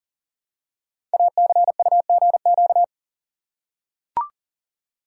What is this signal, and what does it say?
Morse code, heard as a keyed tone, about 39 wpm, 690 Hz. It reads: ACUGQ
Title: Code Groups - 5 Letters - 40wpm